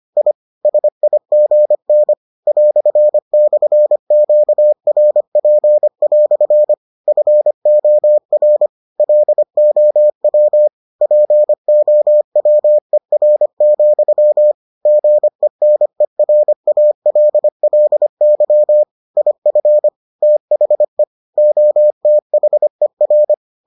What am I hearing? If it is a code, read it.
Morse code, heard as a keyed tone, about 25 wpm, 600 Hz. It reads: I SIGN "/QRP" FOR LOW POWER, GENERALLY IF THE OTHER